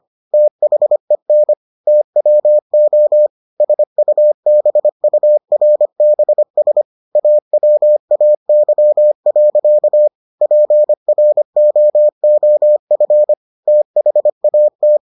Morse code, heard as a keyed tone, about 25 wpm, 600 Hz. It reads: THEN TWO SUBURBS AWAY. PROOF THAT